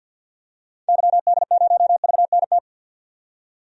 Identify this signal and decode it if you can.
Morse code, heard as a keyed tone, about 51 wpm, 700 Hz. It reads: YD0VNT
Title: DXCC-100 Call Signs - 6 Characters - 50wpm